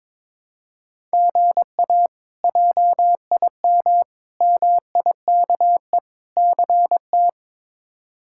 Morse code, heard as a keyed tone, about 22 wpm, 700 Hz. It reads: GA JIM MIKE CT
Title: K1USN SST QSO Practice - 22wpm